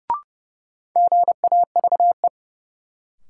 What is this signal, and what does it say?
Morse code, heard as a keyed tone, about 30 wpm, 700 Hz. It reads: GAVE
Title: Sentences from Top 500 Words - New Word Drill - 30wpm